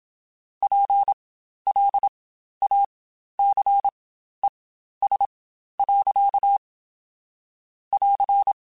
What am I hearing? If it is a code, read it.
Morse code, heard as a keyed tone, about 27 wpm, spaced as for 12 wpm, 800 Hz. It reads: PLACES. <AR>
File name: Spaced / Chuckles Spaced-mp3s / MiddleAge sp @25